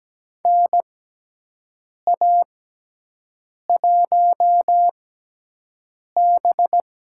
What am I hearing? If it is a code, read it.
Morse code, heard as a keyed tone, about 17 wpm, spaced as for 6 wpm, 700 Hz. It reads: NA1B
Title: USA Call Signs - 4 Characters - 6x Character Spacing - 17wpm